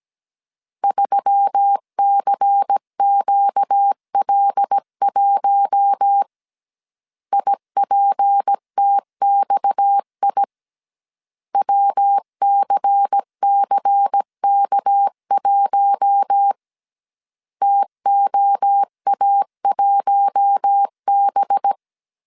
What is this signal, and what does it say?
Morse code, heard as a keyed tone, about 17 wpm, 800 Hz. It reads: 3CQL1 IPTXI WCCK1 TOA1B
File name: CW / DET / 1 6 1500C S A 16wpm 2635